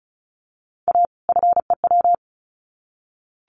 Morse code, heard as a keyed tone, about 34 wpm, 700 Hz. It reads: A FEW
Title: Sets of 2 Words 35wpm